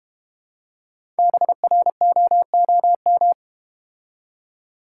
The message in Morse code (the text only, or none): BROOM